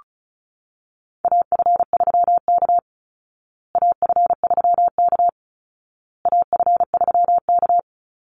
Morse code, heard as a keyed tone, about 34 wpm, 700 Hz. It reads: AF3K AF3K AF3K
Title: Call Signs Repeated 2x - 35wpm